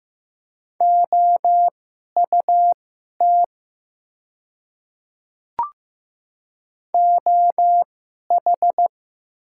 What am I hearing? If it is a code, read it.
Morse code, heard as a keyed tone, about 15 wpm, spaced as for 11 wpm, 700 Hz. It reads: OUT OH
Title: 2x Character Spacing - Top 500 Words - 15wpm